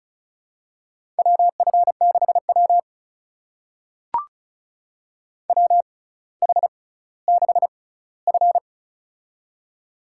Morse code, heard as a keyed tone, about 34 wpm, spaced as for 18 wpm, 700 Hz. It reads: WF6W WH6F
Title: USA Call Signs - 4 Characters - 6x Character Spacing - 35wpm